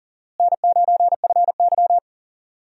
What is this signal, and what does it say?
Morse code, heard as a keyed tone, about 40 wpm, 700 Hz. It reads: N9FY
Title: USA Call Signs - 4 Characters - Repeated 2x - 40wpm